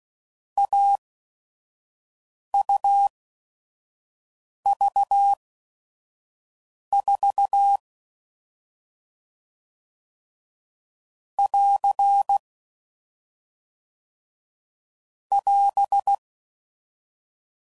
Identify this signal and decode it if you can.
Morse code, heard as a keyed tone, about 16 wpm, spaced as for 5 wpm, 800 Hz. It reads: AUV4 <AR> <AS>
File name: Mixed / Short Runs / AUV AUV4